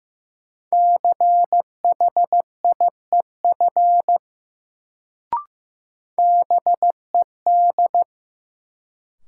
Morse code, heard as a keyed tone, about 15 wpm, 700 Hz. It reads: CHIEF BED
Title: Sets of 5 Words Encore - New Word Drill - 15wpm